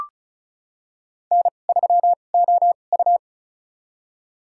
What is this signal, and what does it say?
Morse code, heard as a keyed tone, about 34 wpm, spaced as for 25 wpm, 700 Hz. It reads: N3OU
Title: USA Call Signs - 4 Characters - 2x Character Spacing - 35wpm